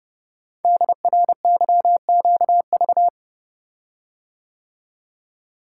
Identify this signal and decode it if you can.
Morse code, heard as a keyed tone, about 30 wpm, 700 Hz. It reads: DRYQV
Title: Code Groups - 5 Letters - 30wpm